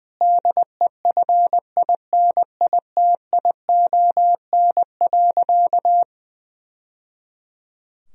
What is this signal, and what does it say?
Morse code, heard as a keyed tone, about 20 wpm, 700 Hz. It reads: DEFINITION.
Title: Sentences from Top 1000 Words - Review Part 4 - 20wpm